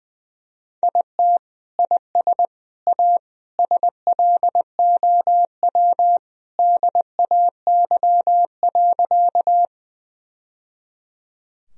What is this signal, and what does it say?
Morse code, heard as a keyed tone, about 20 wpm, 700 Hz. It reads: IT IS A SLOW DAY.
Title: Sentences from Top 1000 Words - Review Part 2 - 20wpm